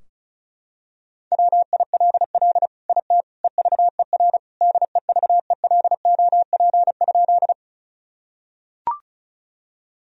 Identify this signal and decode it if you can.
Morse code, heard as a keyed tone, about 36 wpm, 700 Hz. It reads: WILL IT EVER DEVELOP?
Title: Sentences from Top 1000 Words - Review Part 2 - 35wpm